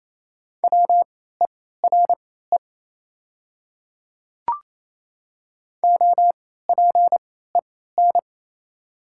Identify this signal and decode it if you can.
Morse code, heard as a keyed tone, about 28 wpm, spaced as for 16 wpm, 700 Hz. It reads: WERE OPEN
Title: 3x Character Spacing - Top 500 Words - 28wpm